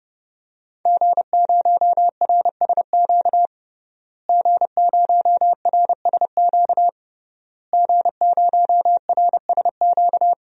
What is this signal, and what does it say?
Morse code, heard as a keyed tone, about 30 wpm, 700 Hz. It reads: G0RSQ G0RSQ G0RSQ
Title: DXCC-100 Call Signs - 5 Characters - Repeated 3x - 30wpm